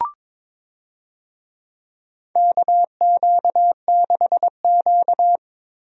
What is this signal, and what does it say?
Morse code, heard as a keyed tone, about 22 wpm, 700 Hz. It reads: KQ6Q
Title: USA Call Signs - 4 Characters - Repeated 3x - 22wpm